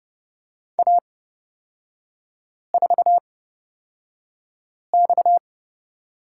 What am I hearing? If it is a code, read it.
Morse code, heard as a keyed tone, about 30 wpm, spaced as for 5 wpm, 690 Hz. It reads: A4X
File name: Lesson 37e - Letter x - Three Characters 30wpm